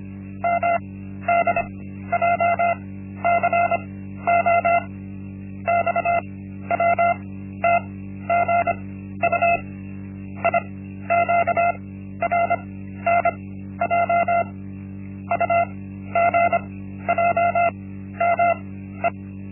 Morse code, heard as a keyed tone, about 25 wpm, spaced as for 13 wpm, 690 Hz. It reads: MDJCO XWTGU IQRNJ UGJME
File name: Linksmakalnis-1